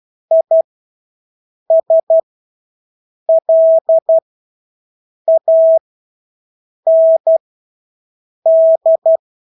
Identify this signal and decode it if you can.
Morse code, heard as a keyed tone, about 12 wpm, spaced as for 6 wpm, 650 Hz. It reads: ISLAND
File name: Engelske ord 12-6 wpm